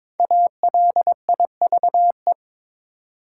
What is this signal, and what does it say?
Morse code, heard as a keyed tone, about 22 wpm, 690 Hz. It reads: ALIVE
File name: 5 Letter Words - First 3 Letters Given - 22wpm